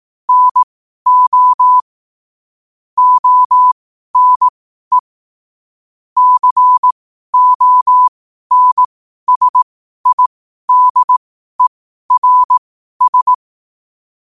Morse code, heard as a keyed tone, about 18 wpm, spaced as for 12 wpm, 1000 Hz. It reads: NO ONE CONSIDERS
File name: qotdfiles / 12wpm